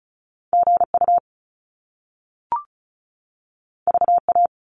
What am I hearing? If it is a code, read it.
Morse code, heard as a keyed tone, about 36 wpm, 700 Hz. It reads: GU VA
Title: Top Two Letter Combinations in Words 35wpm